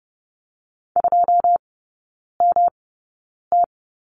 Morse code, heard as a keyed tone, about 30 wpm, spaced as for 9 wpm, 700 Hz. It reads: 2MT
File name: Lesson 18e - Letter m - Three Characters 30wpm